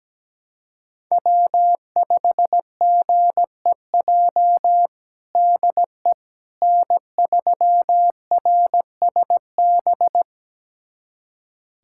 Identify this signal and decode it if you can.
Morse code, heard as a keyed tone, about 17 wpm, 700 Hz. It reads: W5GEJ DE N3RSB
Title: USA Callsign DE Callsign - 5 Characters - 17wpm